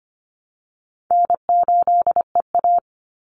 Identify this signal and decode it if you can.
Morse code, heard as a keyed tone, about 25 wpm, 700 Hz. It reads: N8EA
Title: Call Signs - 25wpm